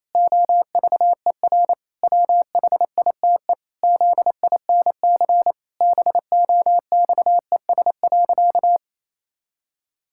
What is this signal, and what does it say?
Morse code, heard as a keyed tone, about 28 wpm, 700 Hz. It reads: OVER WHITE ZINC BOXES.
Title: Pangram Challenge v2 - 28wpm